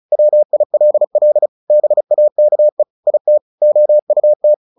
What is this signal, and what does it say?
Morse code, heard as a keyed tone, about 34 wpm, 600 Hz. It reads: WILL BAKE IT OUT
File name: the old man and the sea 2 35wpm